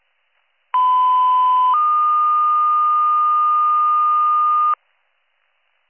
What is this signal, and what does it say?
Two-tone page, heard as a one-second tone, then a three-second tone, about 1010 then 1220 Hz. It is heard from 0.7 to 4.7 s.